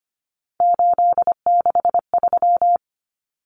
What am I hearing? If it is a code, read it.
Morse code, heard as a keyed tone, about 25 wpm, 700 Hz. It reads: 863
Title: Common Cut Numbers - 3-Digit 25wpm